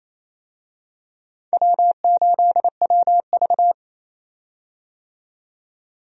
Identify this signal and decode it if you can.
Morse code, heard as a keyed tone, about 28 wpm, 700 Hz. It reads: W8WV
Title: USA Call Signs - 4 Characters - 28wpm